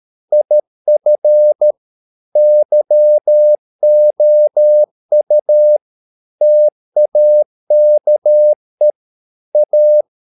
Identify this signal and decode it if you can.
Morse code, heard as a keyed tone, about 13 wpm, 600 Hz. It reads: IF YOU TAKE A
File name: mp3 files / Lexiphiles - 13wpm